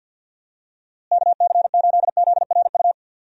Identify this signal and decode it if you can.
Morse code, heard as a keyed tone, about 52 wpm, 700 Hz. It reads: KC8ZRU
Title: USA Call Signs - 6 Characters - Repeated 3x - 50wpm